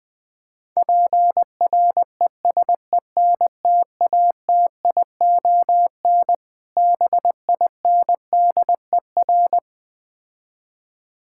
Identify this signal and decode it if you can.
Morse code, heard as a keyed tone, about 20 wpm, 700 Hz. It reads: PRESENTATION BINDER
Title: Difficult Words - Speed-Racing with Hints - 20wpm